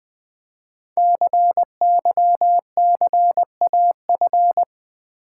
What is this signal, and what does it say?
Morse code, heard as a keyed tone, about 20 wpm, 700 Hz. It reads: CYCAF
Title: Code Groups - 5 Letters - 20wpm